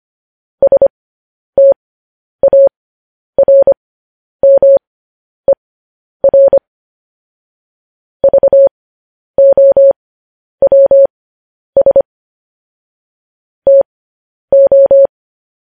Morse code, heard as a keyed tone, about 25 wpm, spaced as for 10 wpm, 550 Hz. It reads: STARMER VOWS TO
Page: News Headlines in Morse Code at 10 WPM